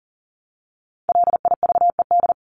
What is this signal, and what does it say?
Morse code, heard as a keyed tone, about 40 wpm, 700 Hz. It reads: LIVED
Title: ED Suffix - Top 100 Words with Base Words 40wpm